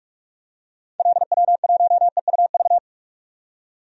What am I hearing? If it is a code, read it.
Morse code, heard as a keyed tone, about 43 wpm, 700 Hz. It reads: LW1EUV
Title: DXCC-100 Call Signs - 6 Characters - 45wpm